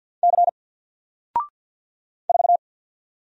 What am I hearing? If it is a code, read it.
Morse code, heard as a keyed tone, about 48 wpm, 700 Hz. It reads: C 4
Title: Single Letter-Number Alt 50wpm